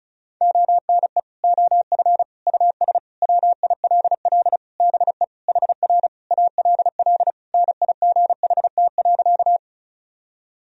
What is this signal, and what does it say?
Morse code, heard as a keyed tone, about 36 wpm, 700 Hz. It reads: ONE OF US WILL BE HR ALL NIGHT.
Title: Sentences from Top 300 Words and Abbreviations 35wpm